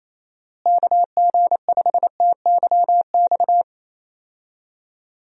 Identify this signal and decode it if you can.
Morse code, heard as a keyed tone, about 27 wpm, 700 Hz. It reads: KG5TYX
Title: USA Call Signs - 6 Characters - Repeated 3x - 28wpm